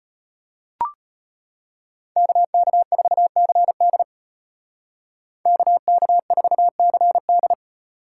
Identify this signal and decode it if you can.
Morse code, heard as a keyed tone, about 33 wpm, 700 Hz. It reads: KK4CD KK4CD